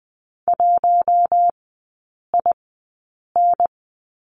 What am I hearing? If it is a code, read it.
Morse code, heard as a keyed tone, about 20 wpm, spaced as for 8 wpm, 700 Hz. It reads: 1IN
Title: 041 Lesson 8e - Number 1 - Three Characters 20wpm